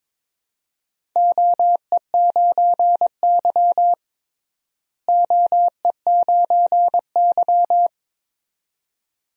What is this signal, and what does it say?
Morse code, heard as a keyed tone, about 22 wpm, 700 Hz. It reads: OE9Y OE9Y
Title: DXCC-100 Call Signs - 4 Characters - Repeated 2x - 22wpm